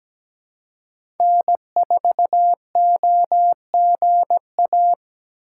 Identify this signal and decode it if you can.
Morse code, heard as a keyed tone, about 17 wpm, 700 Hz. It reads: N4OGA
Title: USA Call Signs - 5 Characters - 3x Character Spacing - 17wpm